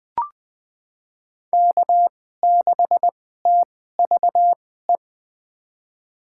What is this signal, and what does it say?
Morse code, heard as a keyed tone, about 21 wpm, spaced as for 15 wpm, 700 Hz. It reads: K6TVE
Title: USA Call Signs - 2x Character Spacing - 20wpm